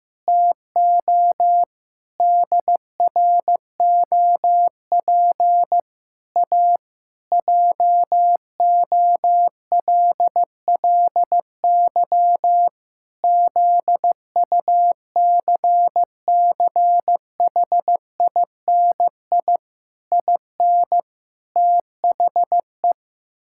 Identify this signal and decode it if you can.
Morse code, heard as a keyed tone, about 16 wpm, 700 Hz. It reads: TO DROP A JOLLY ZUCCHINI IN THE
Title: Pangram Challenge v2 - 15wpm